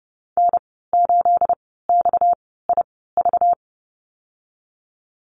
Morse code, heard as a keyed tone, about 32 wpm, spaced as for 17 wpm, 700 Hz. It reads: N8XIV
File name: Lesson 37g - Letter x - Call Signs 30wpm